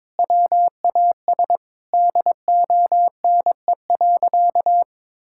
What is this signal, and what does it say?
Morse code, heard as a keyed tone, about 22 wpm, 700 Hz. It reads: WAS DONE.
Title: Sentences from Top 1000 Words - Review Part 5 - 22wpm